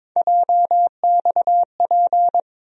Morse code, heard as a keyed tone, about 22 wpm, 700 Hz. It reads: JXP
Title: USA Call Sign Postfixes - 3 Letters 22wpm